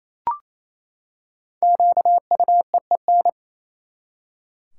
Morse code, heard as a keyed tone, about 28 wpm, 700 Hz. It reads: QUEEN
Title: Five Letter Words - 28wpm